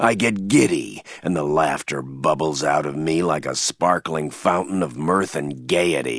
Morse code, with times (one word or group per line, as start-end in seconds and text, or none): none